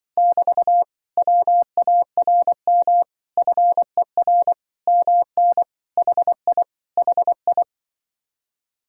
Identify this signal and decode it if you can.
Morse code, heard as a keyed tone, about 24 wpm, 700 Hz. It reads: <BT> WARM FER MN HI HI
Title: Speed-Racing Common QSO Phrases - 20wpm